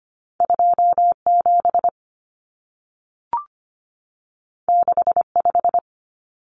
Morse code, heard as a keyed tone, about 25 wpm, 700 Hz. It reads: 27 65
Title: Numbers - 2-Digit 25wpm